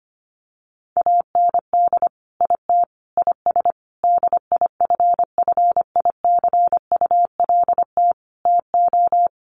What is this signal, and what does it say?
Morse code, heard as a keyed tone, about 25 wpm, 700 Hz. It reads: AND IT IS DIFFICULT TO
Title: Sentences from Top 1100 Words 25wpm